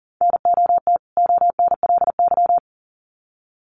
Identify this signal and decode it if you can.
Morse code, heard as a keyed tone, about 40 wpm, 700 Hz. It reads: NOT ONLY
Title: Sets of 2 Words 40wpm